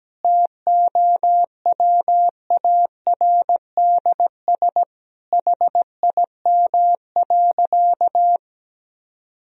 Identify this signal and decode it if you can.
Morse code, heard as a keyed tone, about 17 wpm, 700 Hz. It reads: TOWARDS HIM.
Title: Sentences from Top 1000 Words - Review Part 2 - 17wpm